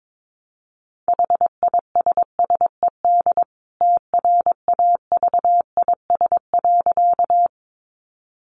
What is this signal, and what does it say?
Morse code, heard as a keyed tone, about 22 wpm, 700 Hz. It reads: HISSED TRAVIS.
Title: A Sound Of Thunder 22wpm